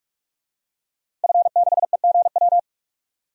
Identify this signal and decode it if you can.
Morse code, heard as a keyed tone, about 42 wpm, 700 Hz. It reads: F6EGW
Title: DXCC-100 Call Signs - 5 Characters - Repeated 3x - 45wpm